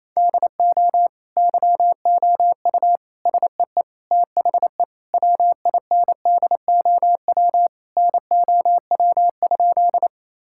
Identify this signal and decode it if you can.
Morse code, heard as a keyed tone, about 28 wpm, 700 Hz. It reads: DO YOU SEE THE WINDOW NOW?
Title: Sentences from Top 1600 Words 28wpm